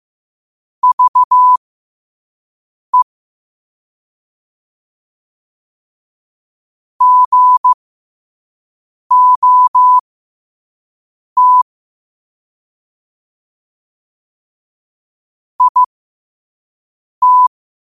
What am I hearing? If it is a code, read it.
Morse code, heard as a keyed tone, about 15 wpm, spaced as for 5 wpm, 1000 Hz. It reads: VE GOT IT
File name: qotdfiles / 5wpm